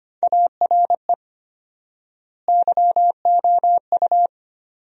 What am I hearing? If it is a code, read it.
Morse code, heard as a keyed tone, about 25 wpm, 700 Hz. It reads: ARE YOU
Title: 4x Word-Spacing - Sentences from Top 400 Words - 25wpm